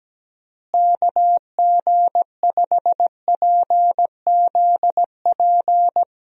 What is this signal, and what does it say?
Morse code, heard as a keyed tone, about 17 wpm, 700 Hz. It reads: KG5PZP
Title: USA Call Signs - 6 Characters - Repeated 3x - 17wpm